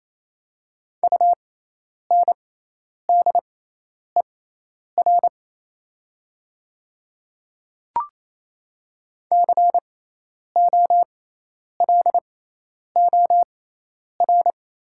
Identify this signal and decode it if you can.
Morse code, heard as a keyed tone, about 28 wpm, spaced as for 10 wpm, 700 Hz. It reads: UNDER COLOR
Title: 6x Character Spacing - Top 300 Words - 28wpm